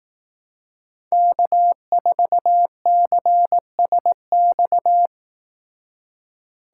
Morse code, heard as a keyed tone, about 18 wpm, 700 Hz. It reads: K4CSX
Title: Speed Racing - USA 1x3 Call Signs 15wpm